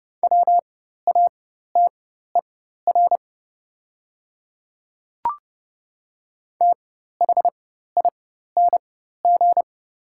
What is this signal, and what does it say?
Morse code, heard as a keyed tone, about 30 wpm, spaced as for 14 wpm, 700 Hz. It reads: WATER THING
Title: 4x Character Spacing - Top 100 Words - 30wpm